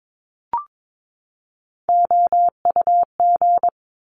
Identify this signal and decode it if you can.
Morse code, heard as a keyed tone, about 22 wpm, 700 Hz. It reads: OUG
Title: Top Three Letter Combinations in Words 22wpm